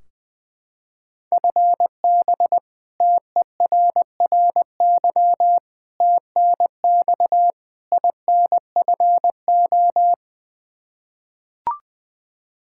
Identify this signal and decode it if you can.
Morse code, heard as a keyed tone, about 20 wpm, 700 Hz. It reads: FB TERRY TNX INFO
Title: Practice QSO - Exchange 3 - More Info - 20wpm